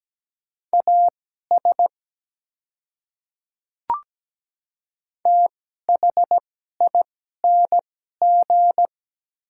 Morse code, heard as a keyed tone, about 17 wpm, spaced as for 12 wpm, 700 Hz. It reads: AS THING
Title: 2x Character Spacing - Top 100 Words - 17wpm